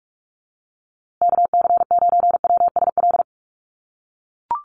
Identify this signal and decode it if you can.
Morse code, heard as a keyed tone, about 44 wpm, 700 Hz. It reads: KC9WSL
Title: USA Call Signs Repeated 2x 45wpm